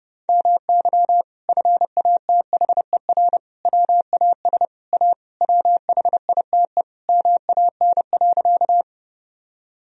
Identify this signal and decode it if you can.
Morse code, heard as a keyed tone, about 31 wpm, 700 Hz. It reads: MY FATHER WAS A WHITE MAN.
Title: Sentences from Top 1000 Words - Review Part 1 - 30wpm